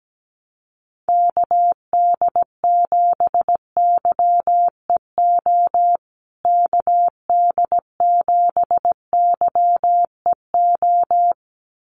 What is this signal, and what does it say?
Morse code, heard as a keyed tone, about 17 wpm, 700 Hz. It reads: KD7YEO KD7YEO
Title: USA Call Signs Repeated 2x 17wpm